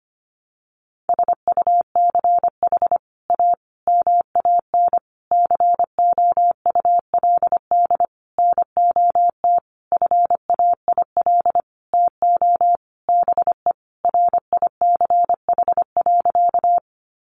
Morse code, heard as a keyed tone, about 25 wpm, 700 Hz. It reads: SUCH A MAN COULD NOT FAIL TO BE RICH.
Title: Sentences from Top 1100 Words 25wpm